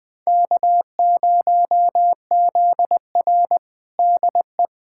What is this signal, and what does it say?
Morse code, heard as a keyed tone, about 20 wpm, 700 Hz. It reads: K0ZR DE
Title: Practice QSO - Exchange 4 - Ending - 20wpm